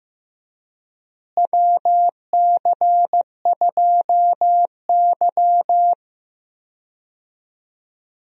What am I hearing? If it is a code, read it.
Morse code, heard as a keyed tone, about 15 wpm, 700 Hz. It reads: WC2Y
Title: USA Call Signs - 4 Characters - Repeated 2x - 15wpm